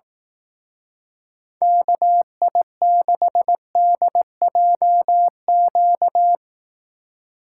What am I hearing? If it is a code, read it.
Morse code, heard as a keyed tone, about 18 wpm, 700 Hz. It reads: KI6DJQ
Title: Speed Racing - USA 2x3 Call Signs 15wpm